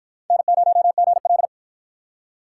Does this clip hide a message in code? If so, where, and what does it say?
Yes: Morse code from 0.3 to 1.5 s, heard as a keyed tone, about 49 wpm, 700 Hz. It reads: N0GL